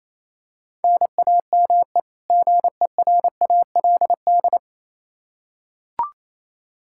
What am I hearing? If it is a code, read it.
Morse code, heard as a keyed tone, about 27 wpm, 700 Hz. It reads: NAME GERALD
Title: Practice QSO - Exchange 1 - Beginning - 28wpm